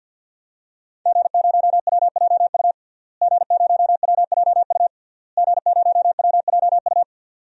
Morse code, heard as a keyed tone, about 46 wpm, 700 Hz. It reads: G0WJU G0WJU G0WJU
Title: DXCC-100 Call Signs - 5 Characters - Repeated 3x - 50wpm